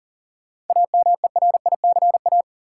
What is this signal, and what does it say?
Morse code, heard as a keyed tone, about 40 wpm, 700 Hz. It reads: AMERICA
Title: Sets of 5 Words - New Word Drill - 40wpm